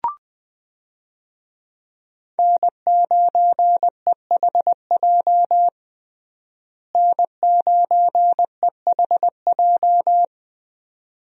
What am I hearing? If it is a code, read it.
Morse code, heard as a keyed tone, about 20 wpm, 700 Hz. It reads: N9EHJ N9EHJ
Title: USA Call Signs - 5 Characters - Repeated 2x - 20wpm